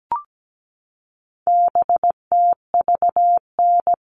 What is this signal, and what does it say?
Morse code, heard as a keyed tone, about 17 wpm, 700 Hz. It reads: BTVN
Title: All Cut Numbers - 4-Digit 17wpm